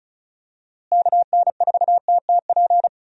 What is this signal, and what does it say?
Morse code, heard as a keyed tone, about 34 wpm, 700 Hz. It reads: KN4TTP
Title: USA Call Signs - 6 Characters - Repeated 2x - 35wpm